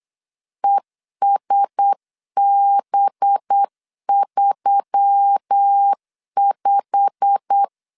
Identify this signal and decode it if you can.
Morse code, heard as a keyed tone, about 8 wpm, 800 Hz. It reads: ESB35
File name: LEZIONI / 8 YJ45 / 8 YJ45 8wpm 3214